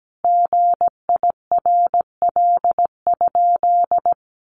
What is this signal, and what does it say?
Morse code, heard as a keyed tone, about 17 wpm, 700 Hz. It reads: GIRL?
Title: Sentences from Top 800 Words 17wpm